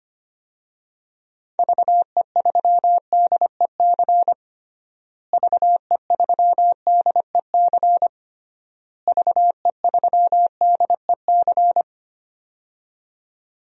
Morse code, heard as a keyed tone, about 25 wpm, 700 Hz. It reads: VE3DEC VE3DEC VE3DEC
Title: DXCC-100 Call Signs - 6 Characters - Repeated 3x - 25wpm